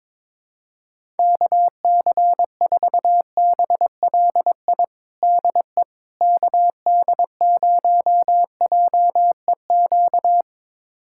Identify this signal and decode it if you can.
Morse code, heard as a keyed tone, about 21 wpm, 700 Hz. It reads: KC4BLI DE KD0JEQ
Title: USA Callsign DE Callsign 22wpm